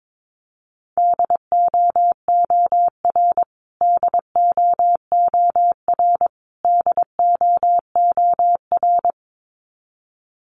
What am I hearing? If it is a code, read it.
Morse code, heard as a keyed tone, about 22 wpm, 700 Hz. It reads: DOOR DOOR DOOR
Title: Top 300 Words Repeated 22wpm